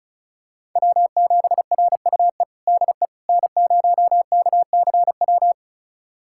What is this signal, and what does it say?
Morse code, heard as a keyed tone, about 36 wpm, 700 Hz. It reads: W7RUE DE N0KCW
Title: USA Callsign DE Callsign - 5 Characters - 35wpm